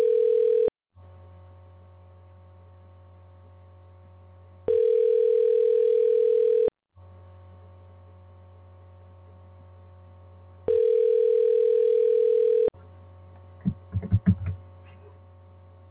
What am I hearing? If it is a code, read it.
Ringback tone (North America), 440 + 480 Hz.